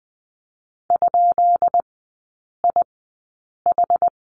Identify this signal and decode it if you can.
Morse code, heard as a keyed tone, about 20 wpm, spaced as for 8 wpm, 700 Hz. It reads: ?IH
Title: 138 Lesson 23e - Letter f - Three Characters 20wpm